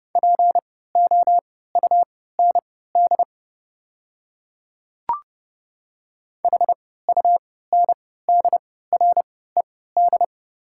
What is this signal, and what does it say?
Morse code, heard as a keyed tone, about 31 wpm, spaced as for 17 wpm, 700 Hz. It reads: POUND HUNDRED
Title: 3x Character Spacing - Top 500 Words - 30wpm